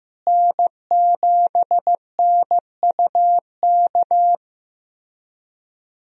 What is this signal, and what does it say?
Morse code, heard as a keyed tone, about 15 wpm, 700 Hz. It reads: N7NUK
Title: USA Call Signs - 5 Characters - Repeated 3x - 15wpm